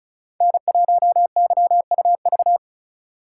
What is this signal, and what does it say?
Morse code, heard as a keyed tone, about 36 wpm, 700 Hz. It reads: N1YUV
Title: USA Call Signs - 5 Characters - 4x Character Spacing - 35wpm